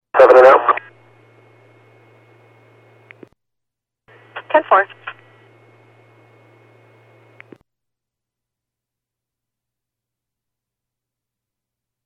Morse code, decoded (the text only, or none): none